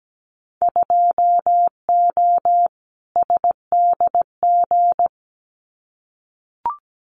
Morse code, heard as a keyed tone, about 17 wpm, 700 Hz. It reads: 2O SDG
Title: Winter Field Day Exchanges 17wpm